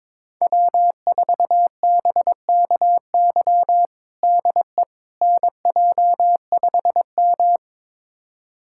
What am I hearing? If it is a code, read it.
Morse code, heard as a keyed tone, about 22 wpm, 700 Hz. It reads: W4BKY DE NJ5M
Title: USA Callsign DE Callsign 22wpm